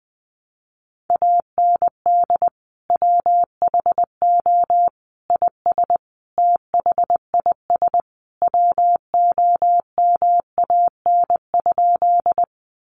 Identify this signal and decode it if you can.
Morse code, heard as a keyed tone, about 20 wpm, 700 Hz. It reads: AND WHO IS THIS WOMAN?
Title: Sentences from Top 1100 Words 20wpm